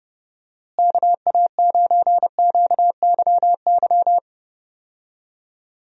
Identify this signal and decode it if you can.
Morse code, heard as a keyed tone, about 31 wpm, 700 Hz. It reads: KA9QYY
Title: USA Call Signs - 6 Characters - Repeated 2x - 30wpm